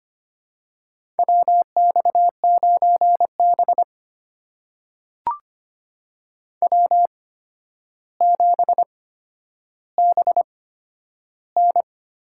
Morse code, heard as a keyed tone, about 25 wpm, spaced as for 11 wpm, 700 Hz. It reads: WX9B W7BN
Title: USA Call Signs - 4 Characters - 8x Character Spacing - 25wpm